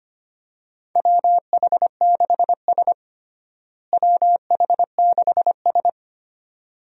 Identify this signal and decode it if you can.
Morse code, heard as a keyed tone, about 25 wpm, 700 Hz. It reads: WH6S WH6S
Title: USA Call Signs - 4 Characters - Repeated 2x - 25wpm